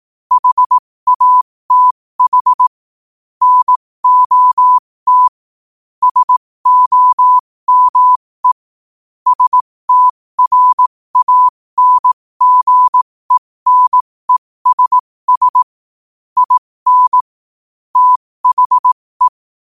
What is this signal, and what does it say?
Morse code, heard as a keyed tone, about 18 wpm, spaced as for 15 wpm, 1000 Hz. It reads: HATH NOT SOME STRANGENESS IN THE
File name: qotdfiles / 15wpm